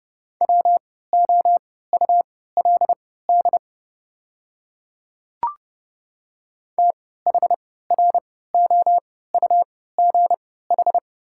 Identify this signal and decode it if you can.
Morse code, heard as a keyed tone, about 30 wpm, spaced as for 17 wpm, 700 Hz. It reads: WOULD THROUGH
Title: 3x Character Spacing - Top 300 Words - 30wpm